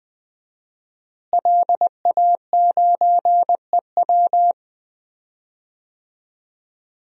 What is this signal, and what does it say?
Morse code, heard as a keyed tone, about 20 wpm, 700 Hz. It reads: LA9EW
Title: DXCC-100 Call Signs - 5 Characters - Repeated 2x - 20wpm